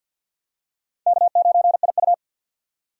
Morse code, heard as a keyed tone, about 49 wpm, 700 Hz. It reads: K9IU
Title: USA Call Signs - 4 Characters - 6x Character Spacing - 50wpm